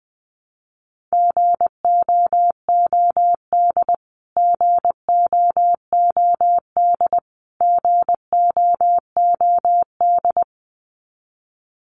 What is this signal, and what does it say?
Morse code, heard as a keyed tone, about 20 wpm, 700 Hz. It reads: GOOD GOOD GOOD
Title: Top 500 Words Repeated 20wpm